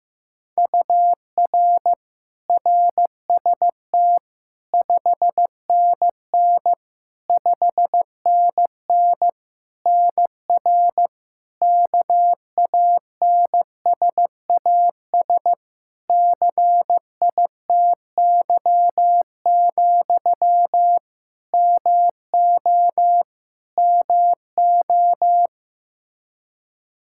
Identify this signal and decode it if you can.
Morse code, heard as a keyed tone, about 15 wpm, 700 Hz. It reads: UR RST 5NN 5NN NR KANSAS CITY, MO MO
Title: Practice QSO - Exchange 1 - Beginning - 15wpm